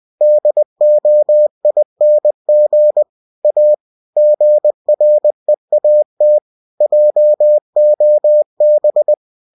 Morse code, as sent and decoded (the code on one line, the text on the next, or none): -.. --- .. -. --. / .- / --. .-. .- - / .--- --- -...
DOING A GREAT JOB